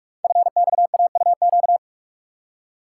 Morse code, heard as a keyed tone, about 44 wpm, 700 Hz. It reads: FXAUQ
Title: Code Groups - 5 Letters - 45wpm